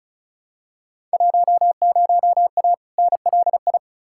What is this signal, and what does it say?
Morse code, heard as a keyed tone, about 36 wpm, 690 Hz. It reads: 10A NLI
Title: Field Day Exchanges Encore - 35wpm